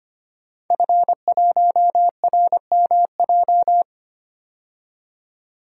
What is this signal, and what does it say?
Morse code, heard as a keyed tone, about 25 wpm, 700 Hz. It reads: F1RMJ
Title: DXCC-100 Call Signs - 5 Characters - 25wpm